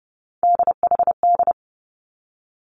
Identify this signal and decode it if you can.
Morse code, heard as a keyed tone, about 30 wpm, 700 Hz. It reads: DHD
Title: Three Challenging Letter-Number Combinations 30wpm